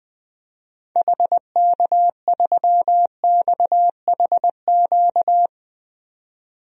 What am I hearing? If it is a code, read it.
Morse code, heard as a keyed tone, about 20 wpm, 700 Hz. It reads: HK3XHQ
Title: DXCC-100 Call Signs - 6 Characters - 20wpm